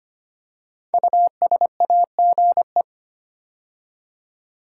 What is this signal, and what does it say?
Morse code, heard as a keyed tone, about 25 wpm, 700 Hz. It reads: USAGE
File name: 5 Letter Words - First 3 Letters Given - 25wpm